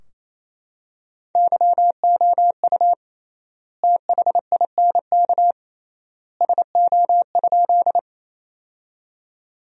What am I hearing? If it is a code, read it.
Morse code, heard as a keyed tone, about 27 wpm, 700 Hz. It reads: YOU THINK SO?
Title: 3x Word-Spacing - Sentences from Top 200 Words - 28wpm